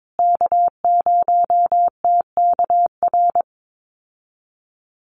Morse code, heard as a keyed tone, about 22 wpm, 700 Hz. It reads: K0TKR
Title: USA Call Signs Repeated 2x 22wpm